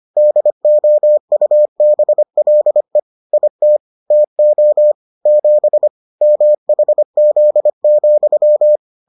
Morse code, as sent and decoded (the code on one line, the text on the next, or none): -.. --- ..- -... .-.. / .. - / - --- / --... / -- .... --.. --..--
DOUBLE IT TO 7 MHZ,